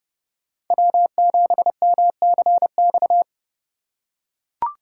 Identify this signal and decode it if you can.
Morse code, heard as a keyed tone, about 30 wpm, 700 Hz. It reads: W7MCX